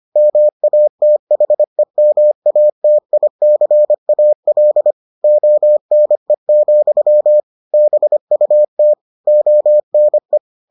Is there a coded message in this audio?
Yes: Morse code from 0.2 to 10.4 s, heard as a keyed tone, about 25 wpm, 600 Hz. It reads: MATHEMATICAL ONE, BUT ONE